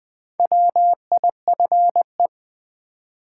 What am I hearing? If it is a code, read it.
Morse code, heard as a keyed tone, about 20 wpm, 700 Hz. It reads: WIFE
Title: Build A Word - 2 Letters At A Time - 4-Letter Words - 20wpm